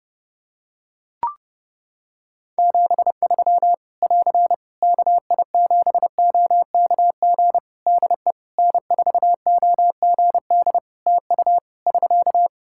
Morse code, heard as a keyed tone, about 30 wpm, 700 Hz. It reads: 73 <AR> KI7OKG DE N4OGD TU <SK>
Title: K1USN SST QSO Practice 30wpm